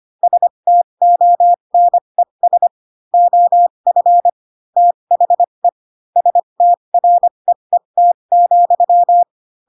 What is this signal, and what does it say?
Morse code, heard as a keyed tone, about 25 wpm, spaced as for 22 wpm, 700 Hz. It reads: STONES OF THE STREET,
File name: Pinocchio / Pinocchio ch3 18 25 ik1hge